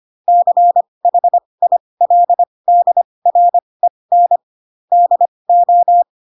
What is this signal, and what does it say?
Morse code, heard as a keyed tone, about 25 wpm, spaced as for 20 wpm, 700 Hz. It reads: CHILDREN DO
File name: Pinocchio / Pinocchio ch4 15 25 ik1hge